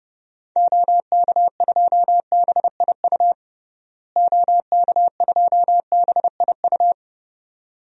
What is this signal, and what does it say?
Morse code, heard as a keyed tone, about 30 wpm, 700 Hz. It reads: OK2BIU OK2BIU
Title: DXCC-100 Call Signs - 6 Characters - Repeated 2x - 30wpm